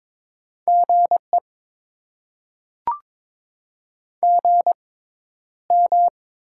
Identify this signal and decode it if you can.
Morse code, heard as a keyed tone, about 22 wpm, spaced as for 11 wpm, 700 Hz. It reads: GE GM
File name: Top 20 QSO Elements - 6x Character Spacing - 22wpm